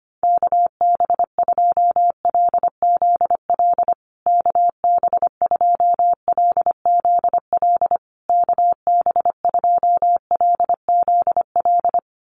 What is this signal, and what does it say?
Morse code, heard as a keyed tone, about 25 wpm, 700 Hz. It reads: KB2LZL KB2LZL KB2LZL
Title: USA Call Signs Repeated 3x 25wpm